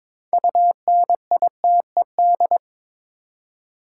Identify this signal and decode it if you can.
Morse code, heard as a keyed tone, about 22 wpm, 700 Hz. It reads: UNITED
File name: Build A Phrase - 2-Word Phrases - 22wpm